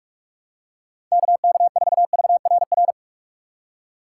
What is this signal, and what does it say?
Morse code, heard as a keyed tone, about 46 wpm, 700 Hz. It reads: KK4VRR
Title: USA Call Signs - 6 Characters - 6x Character Spacing - 45wpm